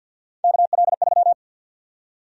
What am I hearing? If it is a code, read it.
Morse code, heard as a keyed tone, about 52 wpm, 700 Hz. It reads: KL3
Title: USA Call Sign Prefixes - 50wpm